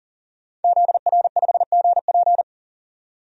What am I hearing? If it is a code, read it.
Morse code, heard as a keyed tone, about 40 wpm, 700 Hz. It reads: ZR5GP
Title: DXCC-100 Call Signs - 5 Characters - 40wpm